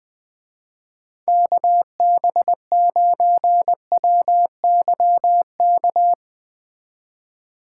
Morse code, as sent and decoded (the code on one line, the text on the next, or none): -.- -... ----. .-- -.-- -.-
KB9WYK